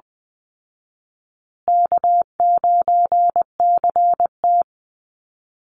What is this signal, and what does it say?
Morse code, heard as a keyed tone, about 20 wpm, 700 Hz. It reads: K9CT
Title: Call Signs Repeated 3x - 20wpm